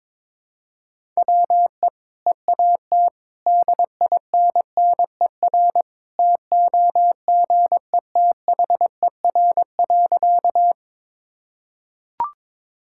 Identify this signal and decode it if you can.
Morse code, heard as a keyed tone, about 22 wpm, 700 Hz. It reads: WE EAT DINNER TOGETHER.
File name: practice / 22wpm / Easy Sentences with Hints - 22wpm